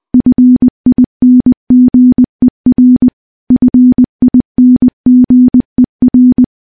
Morse code, heard as a keyed tone, about 20 wpm, 250 Hz. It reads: FINGER FINGER
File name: morse / morse-training-20WPM